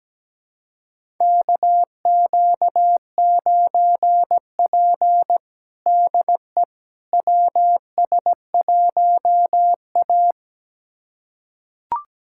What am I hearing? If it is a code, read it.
Morse code, heard as a keyed tone, about 17 wpm, 700 Hz. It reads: KQ9P DE WS1A
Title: USA Callsign DE Callsign - 4 Characters - 17wpm